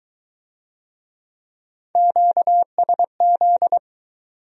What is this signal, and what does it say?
Morse code, heard as a keyed tone, about 23 wpm, 700 Hz. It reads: QSZ